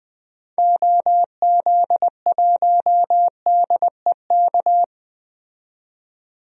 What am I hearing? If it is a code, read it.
Morse code, heard as a keyed tone, about 20 wpm, 700 Hz. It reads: OZ1DEK
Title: DXCC-100 Call Signs - 6 Characters - 20wpm